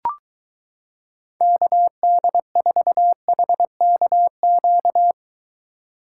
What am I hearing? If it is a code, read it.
Morse code, heard as a keyed tone, about 23 wpm, 700 Hz. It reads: KD4HKQ